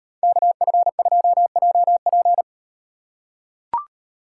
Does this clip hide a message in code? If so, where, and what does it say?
Yes: Morse code from 0.2 to 2.4 s, heard as a keyed tone, about 36 wpm, 700 Hz. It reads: KF2JP